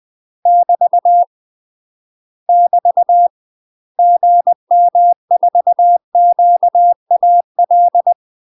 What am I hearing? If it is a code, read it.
Morse code, heard as a keyed tone, about 20 wpm, 700 Hz. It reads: <BT> <BT> GM4QAL
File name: UK MASTER QSOs TRAINING/20WPM